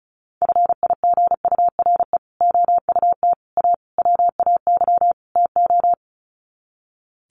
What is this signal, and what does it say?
Morse code, heard as a keyed tone, about 34 wpm, 700 Hz. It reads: FIGURE OUT A WAY TO
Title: Sets of 5 Words Encore2 35wpm